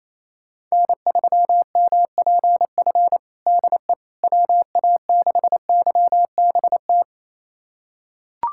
Morse code, heard as a keyed tone, about 28 wpm, 700 Hz. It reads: N3MPF DE WA6YBT
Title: USA Callsign DE Callsign 28wpm